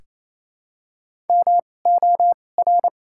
Morse code, heard as a keyed tone, about 28 wpm, spaced as for 20 wpm, 700 Hz. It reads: MOR
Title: 2x Character Spacing - Top 200 Words - 28wpm